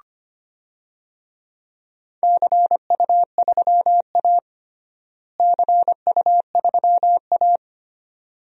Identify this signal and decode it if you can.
Morse code, heard as a keyed tone, about 25 wpm, 700 Hz. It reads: CU3A CU3A
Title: DXCC-100 Call Signs - 4 Characters - Repeated 2x - 25wpm